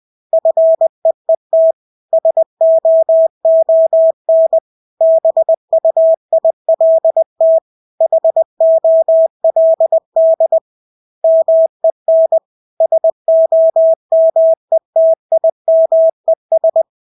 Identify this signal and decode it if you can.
Morse code, heard as a keyed tone, about 20 wpm, 650 Hz. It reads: FEET SOON BUILT HOLD MEN SOMETIMES
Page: Engelske ord 20 wpm